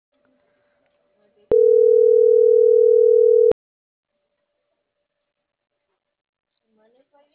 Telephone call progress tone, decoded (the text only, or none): ringback tone